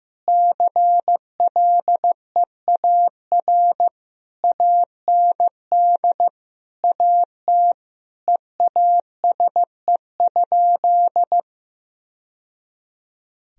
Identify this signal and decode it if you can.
Morse code, heard as a keyed tone, about 15 wpm, 700 Hz. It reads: CLEAR AND AT EASE?
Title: Sentences from Top 1000 Words - Review Part 1 - 15wpm